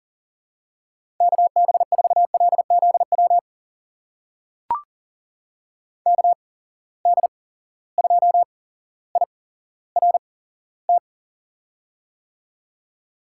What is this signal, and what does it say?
Morse code, heard as a keyed tone, about 39 wpm, spaced as for 17 wpm, 700 Hz. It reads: KB4LZW KD2IRT
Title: USA Call Signs - 6 Characters - 8x Character Spacing - 40wpm